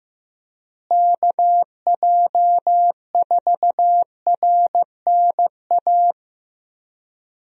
Morse code, heard as a keyed tone, about 15 wpm, 700 Hz. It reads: KJ4RNA
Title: USA Call Signs - 6 Characters - Repeated 2x - 15wpm